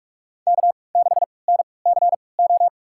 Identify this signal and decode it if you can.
Morse code, heard as a keyed tone, about 46 wpm, spaced as for 26 wpm, 700 Hz. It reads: K6NCO
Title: USA Call Signs - 5 Characters - 3x Character Spacing - 45wpm